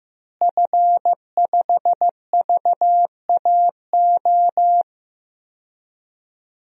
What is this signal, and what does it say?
Morse code, heard as a keyed tone, about 15 wpm, 700 Hz. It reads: F5VAO
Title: DXCC-100 Call Signs - 5 Characters - Repeated 3x - 15wpm